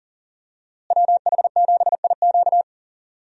Morse code, heard as a keyed tone, about 39 wpm, 700 Hz. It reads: WH7IQ
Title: USA Call Signs - 5 Characters - Repeated 3x - 40wpm